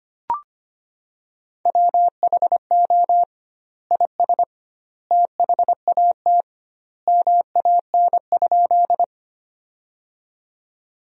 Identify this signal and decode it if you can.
Morse code, heard as a keyed tone, about 25 wpm, 700 Hz. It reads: WHO IS THAT MAN?
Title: 2x Word-Spacing - Sentences from Top 200 Words - 25wpm